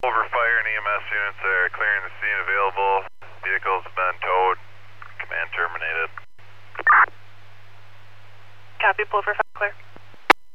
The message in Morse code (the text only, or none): none